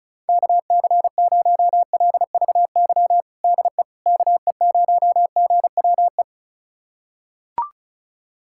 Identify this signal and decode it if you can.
Morse code, heard as a keyed tone, about 34 wpm, 700 Hz. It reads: KC0LVY DE KE0GWE
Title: USA Callsign DE Callsign 35wpm